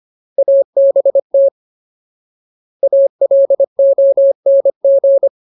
Morse code, heard as a keyed tone, about 25 wpm, 550 Hz. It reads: ABT ALONG
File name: Ultimate Word List / 25WPM Ultimate